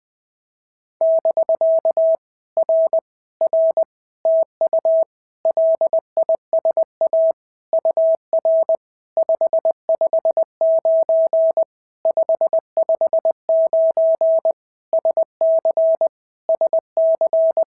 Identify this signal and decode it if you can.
Morse code, heard as a keyed tone, about 20 wpm, 650 Hz. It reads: <BK> R R TU LISA UR 559 559 SC SC
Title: POTA QSO Practice - 20wpm